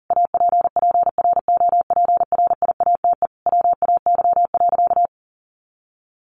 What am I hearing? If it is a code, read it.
Morse code, heard as a keyed tone, about 40 wpm, 700 Hz. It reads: APPROPRIATE WAY.
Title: Sentences from Top 900 Words 40wpm